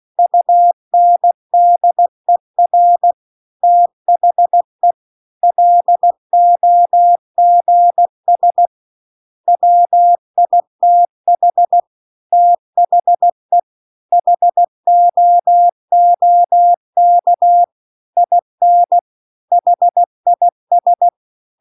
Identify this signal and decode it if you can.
Morse code, heard as a keyed tone, about 16 wpm, 700 Hz. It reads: UNDER THE LOGS WITH THE HOOK IN HIS
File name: CW-The-Big-Twoheaded-River-2-16WPM